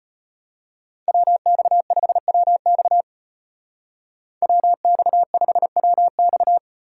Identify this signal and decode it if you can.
Morse code, heard as a keyed tone, about 33 wpm, 700 Hz. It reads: WX5WX WX5WX